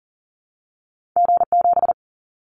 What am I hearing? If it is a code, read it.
Morse code, heard as a keyed tone, about 40 wpm, 700 Hz. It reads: G7
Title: Callsign Prefixes for Top 25 DXCC - Repeated 3x - 40wpm